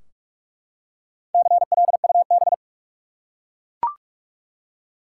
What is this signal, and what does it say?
Morse code, heard as a keyed tone, about 44 wpm, 700 Hz. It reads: CLUB
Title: Four Letter Words - 45wpm